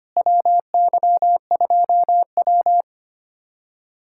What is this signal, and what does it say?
Morse code, heard as a keyed tone, about 25 wpm, 700 Hz. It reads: WY2W
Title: USA Callsign DE Callsign - 4 Characters - 25wpm